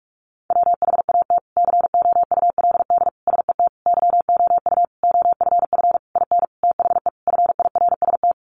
Morse code, heard as a keyed tone, about 44 wpm, 700 Hz. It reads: WHAT COULD SET YOU OFF IN THE FIRST
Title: Sentences from Top 200 Words 45wpm